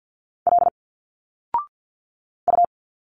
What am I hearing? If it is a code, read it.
Morse code, heard as a keyed tone, about 51 wpm, 700 Hz. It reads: L U
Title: Single Letters 50wpm